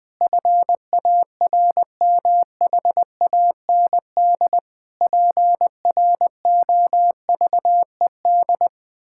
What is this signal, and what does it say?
Morse code, heard as a keyed tone, about 19 wpm, 700 Hz. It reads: FARMHAND PROVED
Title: Pangram Challenge v2 - 20wpm